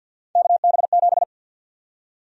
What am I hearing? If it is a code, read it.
Morse code, heard as a keyed tone, about 52 wpm, 700 Hz. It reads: KB7
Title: USA Call Sign Prefixes - 50wpm